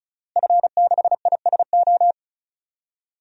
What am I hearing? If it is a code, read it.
Morse code, heard as a keyed tone, about 34 wpm, 700 Hz. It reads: F6ISO
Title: DXCC-100 Call Signs - 5 Characters - Repeated 3x - 35wpm